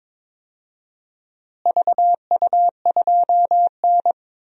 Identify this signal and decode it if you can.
Morse code, heard as a keyed tone, about 22 wpm, 700 Hz. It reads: VU2N
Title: DXCC-100 Call Signs - 4 Characters - Repeated 3x - 22wpm